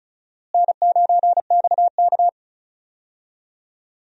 Morse code, heard as a keyed tone, about 36 wpm, 700 Hz. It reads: N9XK
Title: USA Call Signs - 4 Characters - Repeated 2x - 35wpm